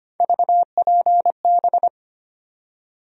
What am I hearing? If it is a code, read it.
Morse code, heard as a keyed tone, about 25 wpm, 700 Hz. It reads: VPB
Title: USA Call Sign Postfixes - 3 Letters 25wpm